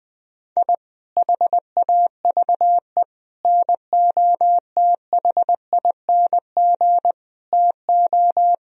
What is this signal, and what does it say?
Morse code, heard as a keyed tone, about 21 wpm, 700 Hz. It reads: I HAVE NOTHING TO
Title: Sentences from Top 1000 Words - Review Part 1 - 20wpm